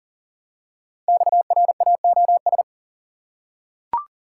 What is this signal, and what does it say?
Morse code, heard as a keyed tone, about 40 wpm, 700 Hz. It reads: XRAOS
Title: Code Groups - 5 Letters - 40wpm